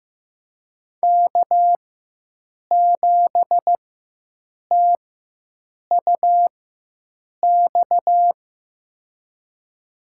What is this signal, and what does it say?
Morse code, heard as a keyed tone, about 15 wpm, spaced as for 7 wpm, 700 Hz. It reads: K7TUX
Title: USA Call Signs - 5 Characters - 4x Character Spacing - 15wpm